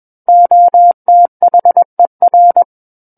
Morse code, heard as a keyed tone, about 21 wpm, 700 Hz. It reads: OTHER
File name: Book One / 12-Chapter 12 - 21 WPM